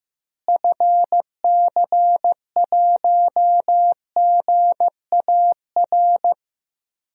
Morse code, heard as a keyed tone, about 15 wpm, 700 Hz. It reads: FC1GAR
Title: DXCC-100 Call Signs - 6 Characters - Repeated 3x - 15wpm